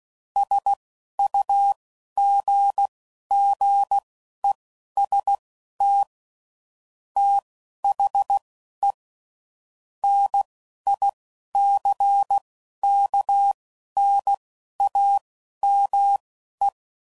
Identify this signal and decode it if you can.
Morse code, heard as a keyed tone, about 16 wpm, spaced as for 11 wpm, 800 Hz. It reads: SUGGEST THE NICKNAME
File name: CW Code Groups @10WPM/Technical Subjects